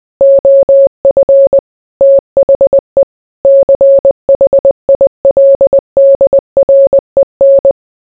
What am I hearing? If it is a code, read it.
Morse code, heard as a keyed tone, about 20 wpm, 550 Hz. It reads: OF THE CHILDREN